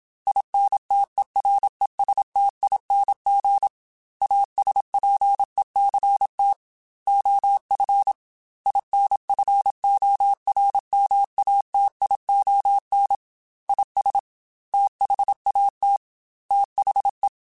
Morse code, heard as a keyed tone, about 27 wpm, 800 Hz. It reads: INTERESTING ASPECT OF INFORMATION IS THAT THE
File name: Technical Subjects / Technical Subjects-mp3s / Sw Pwr Sup @25